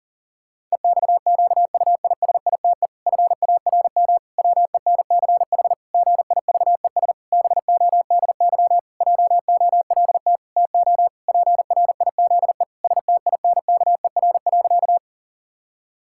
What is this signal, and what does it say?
Morse code, heard as a keyed tone, about 40 wpm, 700 Hz. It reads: EXQUISITE FARM WENCH GIVES BODY JOLT TO PRIZE STINKER.
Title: Pangram Challenge v2 - 40wpm